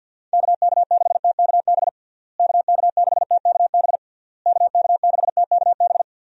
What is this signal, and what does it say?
Morse code, heard as a keyed tone, about 50 wpm, 700 Hz. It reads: KK6TKB KK6TKB KK6TKB
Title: USA Call Signs - 6 Characters - Repeated 3x - 50wpm